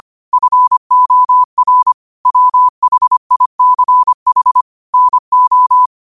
Morse code, heard as a keyed tone, about 25 wpm, 1000 Hz. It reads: FOR WHICH NO